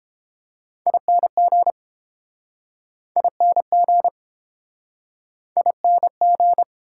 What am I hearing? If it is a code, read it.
Morse code, heard as a keyed tone, about 30 wpm, 700 Hz. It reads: ING ING ING